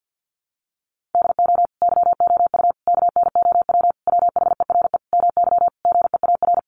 Morse code, heard as a keyed tone, about 50 wpm, 700 Hz. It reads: DO YOU KNOW WHERE MY GEAR
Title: Sentences from Top 1400 Words 50wpm